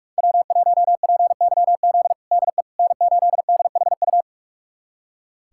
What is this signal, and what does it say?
Morse code, heard as a keyed tone, about 46 wpm, 700 Hz. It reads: W1PYZ DE N8DHU
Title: USA Callsign DE Callsign - 5 Characters - 45wpm